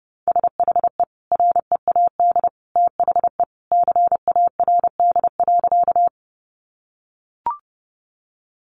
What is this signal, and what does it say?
Morse code, heard as a keyed tone, about 30 wpm, 700 Hz. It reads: SHE READ THE CARD.
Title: Sentences from Top 600 Words 30wpm